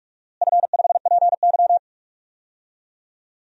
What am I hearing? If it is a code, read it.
Morse code, heard as a keyed tone, about 45 wpm, 700 Hz. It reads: F5PY
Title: DXCC-100 Call Signs - 4 Characters - Repeated 3x - 45wpm